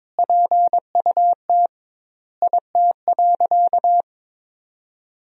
Morse code, heard as a keyed tone, about 21 wpm, 700 Hz. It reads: PUT IT.
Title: 2x Word-Spacing - Sentences from Top 200 Words - 22wpm